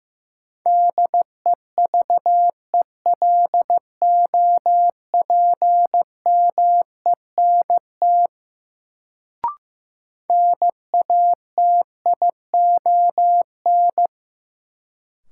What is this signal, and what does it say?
Morse code, heard as a keyed tone, about 15 wpm, 700 Hz. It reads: DEVELOPMENT NATION
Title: Sets of 3 Words - New Word Drill - 15wpm